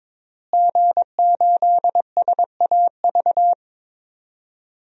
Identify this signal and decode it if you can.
Morse code, heard as a keyed tone, about 22 wpm, 700 Hz. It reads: G8SAV
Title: DXCC-100 Call Signs - 5 Characters - Repeated 2x - 22wpm